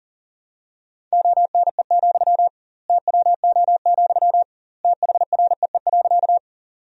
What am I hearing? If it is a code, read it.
Morse code, heard as a keyed tone, about 40 wpm, 700 Hz. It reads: ONE, TWO, THREE.
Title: 2x Word-Spacing - Sentences from Top 200 Words - 40wpm